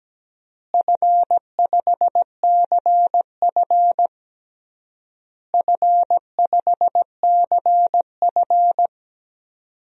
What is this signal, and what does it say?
Morse code, heard as a keyed tone, about 17 wpm, 700 Hz. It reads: F5CF F5CF
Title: DXCC-100 Call Signs - 4 Characters - Repeated 3x - 17wpm